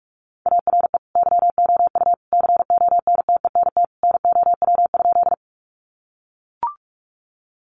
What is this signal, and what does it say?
Morse code, heard as a keyed tone, about 44 wpm, 700 Hz. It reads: ARE YOU CONTENT NOW?
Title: Sentences from Top 900 Words 45wpm